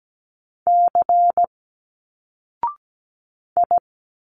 Morse code, heard as a keyed tone, about 17 wpm, 700 Hz. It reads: C I
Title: Single Letters 17wpm